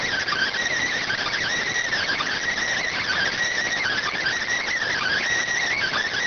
Emergency Alert System (SAME) tones (header bursts), heard as harsh buzzing bursts. It is heard from 0.8 to 5.2 s.